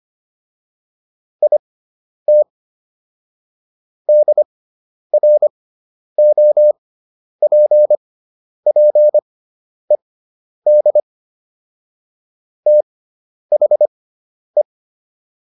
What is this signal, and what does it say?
Morse code, heard as a keyed tone, about 25 wpm, spaced as for 10 wpm, 600 Hz. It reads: IT DROPPED THE